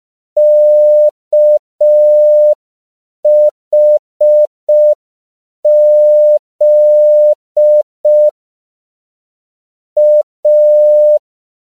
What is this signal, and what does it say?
Morse code, heard as a keyed tone, about 5 wpm, 600 Hz. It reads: KHZ A
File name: CW / 5PPM